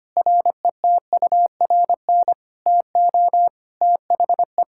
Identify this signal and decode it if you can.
Morse code, heard as a keyed tone, about 25 wpm, 700 Hz. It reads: RETURN TO THE
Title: Sentences from Top 1600 Words 25wpm